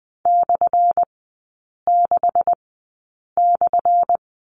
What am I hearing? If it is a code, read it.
Morse code, heard as a keyed tone, about 20 wpm, spaced as for 8 wpm, 700 Hz. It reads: /6/
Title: 178 Lesson 29e - Slash - Three Characters 20wpm